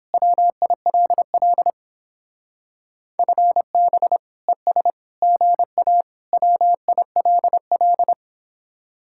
Morse code, heard as a keyed tone, about 29 wpm, 700 Hz. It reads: WILL FB ES GA WILL